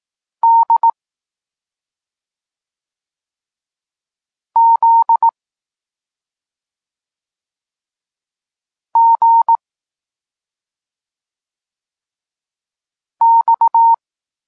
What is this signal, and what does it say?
Morse code, heard as a keyed tone, about 18 wpm, 920 Hz. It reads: D Z G X